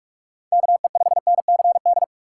Morse code, heard as a keyed tone, about 47 wpm, 700 Hz. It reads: KE5NCD
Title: USA Call Signs - 6 Characters - 3x Character Spacing - 45wpm